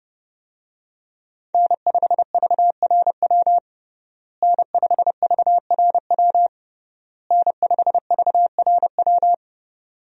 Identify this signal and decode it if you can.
Morse code, heard as a keyed tone, about 30 wpm, 700 Hz. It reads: N5VRW N5VRW N5VRW
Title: USA Call Signs - 5 Characters - Repeated 3x - 30wpm